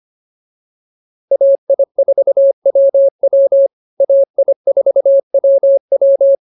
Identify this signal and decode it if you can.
Morse code, heard as a keyed tone, about 24 wpm, 550 Hz. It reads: AI4WW AI4WW